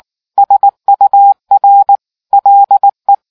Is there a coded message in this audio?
Yes: Morse code from 0.4 to 3.1 s, heard as a keyed tone, about 19 wpm, 800 Hz. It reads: SUR LE